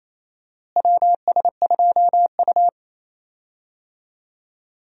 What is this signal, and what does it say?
Morse code, heard as a keyed tone, about 28 wpm, 700 Hz. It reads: WS2U
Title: USA Call Signs - 4 Characters - Repeated 2x - 28wpm